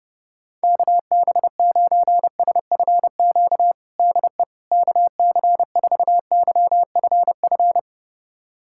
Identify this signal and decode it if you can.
Morse code, heard as a keyed tone, about 30 wpm, 700 Hz. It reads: KB9SFQ DE KC4YFF
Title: USA Callsign DE Callsign - 6 Characters - 30wpm